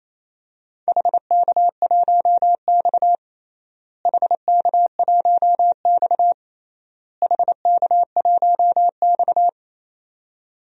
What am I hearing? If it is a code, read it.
Morse code, heard as a keyed tone, about 28 wpm, 700 Hz. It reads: HK1X HK1X HK1X
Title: DXCC-100 Call Signs - 4 Characters - Repeated 3x - 28wpm